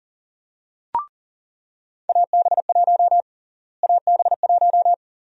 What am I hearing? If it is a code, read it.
Morse code, heard as a keyed tone, about 40 wpm, 700 Hz. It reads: AB1 AB1
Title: USA Call Sign Prefixes - Repeated 2x - 40wpm